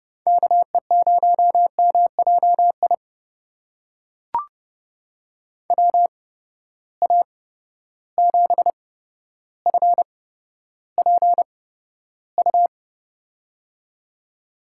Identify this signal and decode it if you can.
Morse code, heard as a keyed tone, about 30 wpm, spaced as for 13 wpm, 700 Hz. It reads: KE0MJI WA7FPU
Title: USA Call Signs - 6 Characters - 8x Character Spacing - 30wpm